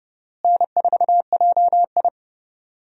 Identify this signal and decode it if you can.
Morse code, heard as a keyed tone, about 30 wpm, 700 Hz. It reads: N4JI
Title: USA Call Signs - 4 Characters - Repeated 2x - 30wpm